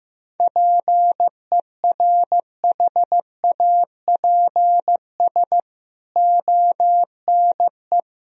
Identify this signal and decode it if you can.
Morse code, heard as a keyed tone, about 15 wpm, 700 Hz. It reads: PERHAPS ONE
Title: Sentences from Top 1000 Words - Review Part 1 - 15wpm